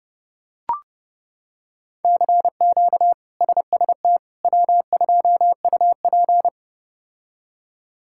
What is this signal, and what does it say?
Morse code, heard as a keyed tone, about 32 wpm, 700 Hz. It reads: CQ SST W2UP
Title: K1USN SST QSO Practice - 30wpm